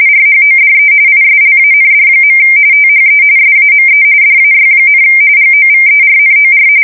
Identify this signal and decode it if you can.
RTTY, about 45.45 baud. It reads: RHODE ISLAND. THE SAME DAY, GOVERNOR
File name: fsk170 / usnavy1